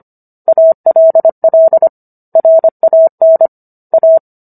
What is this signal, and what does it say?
Morse code, heard as a keyed tone, about 25 wpm, 650 Hz. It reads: ALL RAN A
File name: Book One / 16-Chapter 16 - 25 WPM